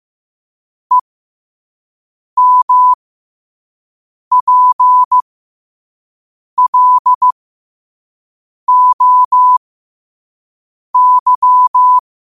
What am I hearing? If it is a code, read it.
Morse code, heard as a keyed tone, about 15 wpm, spaced as for 5 wpm, 1000 Hz. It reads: EMPLOY
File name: qotdfiles / 5wpm